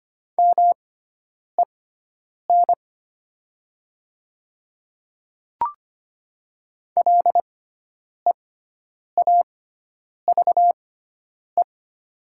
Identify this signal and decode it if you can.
Morse code, heard as a keyed tone, about 25 wpm, spaced as for 9 wpm, 700 Hz. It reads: MEN LEAVE
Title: 6x Character Spacing - Top 500 Words - 25wpm